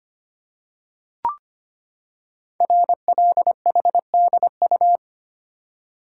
Morse code, heard as a keyed tone, about 25 wpm, 700 Hz. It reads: RLHDU
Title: Code Groups - 5 Letters - 25wpm